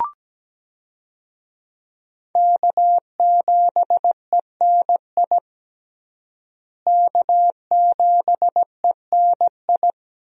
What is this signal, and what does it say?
Morse code, heard as a keyed tone, about 17 wpm, 700 Hz. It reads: K7ENI K7ENI
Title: USA Call Signs - 5 Characters - Repeated 2x - 17wpm